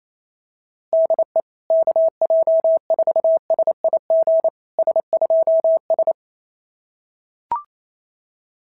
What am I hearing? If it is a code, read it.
Morse code, heard as a keyed tone, about 29 wpm, 650 Hz. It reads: DE KJ4SIG S2S
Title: SOTA QSO Practice 28wpm